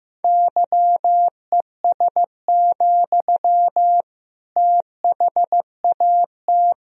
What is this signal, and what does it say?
Morse code, heard as a keyed tone, about 16 wpm, 700 Hz. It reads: YES, THAT
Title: Sentences from Top 1000 Words - Review Part 4 - 15wpm